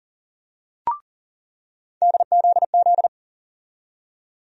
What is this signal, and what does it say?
Morse code, heard as a keyed tone, about 40 wpm, 700 Hz. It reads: DZZ
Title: USA Call Sign Postfixes - 3 Letters 40wpm